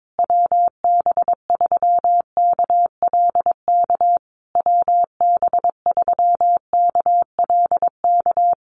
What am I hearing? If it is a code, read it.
Morse code, heard as a keyed tone, about 22 wpm, 700 Hz. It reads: WB3KLK WB3KLK
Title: USA Call Signs Repeated 3x 22wpm